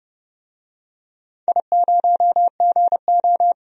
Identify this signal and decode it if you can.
Morse code, heard as a keyed tone, about 31 wpm, 700 Hz. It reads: I0GO
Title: DXCC-100 Call Signs - 4 Characters - 30wpm